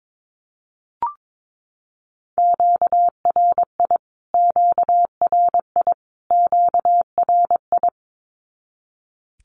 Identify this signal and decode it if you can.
Morse code, heard as a keyed tone, about 22 wpm, 700 Hz. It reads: QRI QRI QRI
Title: Q-Codes Repeated 3x 22wpm